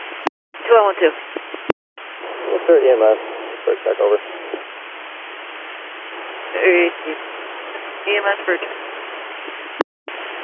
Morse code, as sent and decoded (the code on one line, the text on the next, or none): none